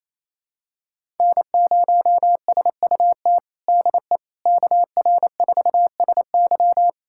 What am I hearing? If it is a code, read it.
Morse code, heard as a keyed tone, about 28 wpm, 700 Hz. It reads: N0SUT DE KR4SY
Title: USA Callsign DE Callsign - 5 Characters - 28wpm